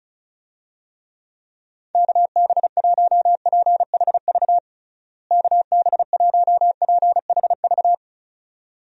Morse code, heard as a keyed tone, about 36 wpm, 700 Hz. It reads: KB1PHV KB1PHV
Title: USA Call Signs - 6 Characters - Repeated 2x - 35wpm